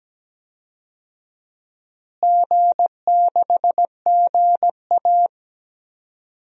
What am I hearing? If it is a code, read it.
Morse code, heard as a keyed tone, about 17 wpm, 700 Hz. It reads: G6GA
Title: DXCC-100 Call Signs - 4 Characters - Repeated 2x - 17wpm